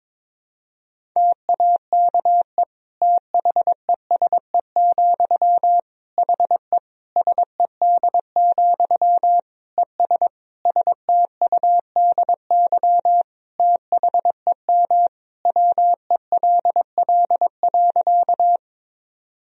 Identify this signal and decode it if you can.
Morse code, heard as a keyed tone, about 21 wpm, 700 Hz. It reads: TAKE THESE, HE SED, ES STUDY THEM WELL.
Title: Sentences from Top 300 Words and Abbreviations 22wpm